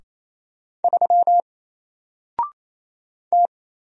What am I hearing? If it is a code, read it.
Morse code, heard as a keyed tone, about 27 wpm, 700 Hz. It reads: 3 T
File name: Single-Letter-Number-28wpm